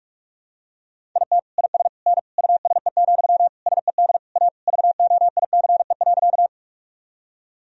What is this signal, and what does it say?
Morse code, heard as a keyed tone, about 41 wpm, 700 Hz. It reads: IT IS N USE, SED A VOICE.
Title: Sentences from Top 500 Words and Abbreviations 45wpm